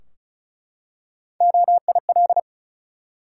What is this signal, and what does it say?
Morse code, heard as a keyed tone, about 36 wpm, 700 Hz. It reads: OIL
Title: Three Letter Words - 35wpm